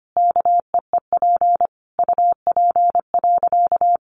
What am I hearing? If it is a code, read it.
Morse code, heard as a keyed tone, about 25 wpm, 700 Hz. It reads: KEEP UP.
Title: Sentences from Top 700 Words 25wpm